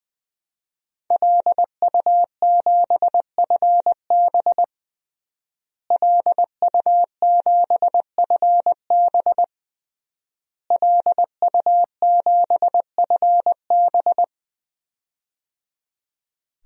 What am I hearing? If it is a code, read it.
Morse code, heard as a keyed tone, about 20 wpm, 700 Hz. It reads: LU7FB LU7FB LU7FB
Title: DXCC-100 Call Signs - 5 Characters - Repeated 3x - 20wpm